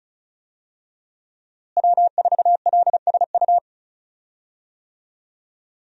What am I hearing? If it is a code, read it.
Morse code, heard as a keyed tone, about 36 wpm, 700 Hz. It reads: W4LSU
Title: USA Call Signs - 5 Characters - Repeated 3x - 35wpm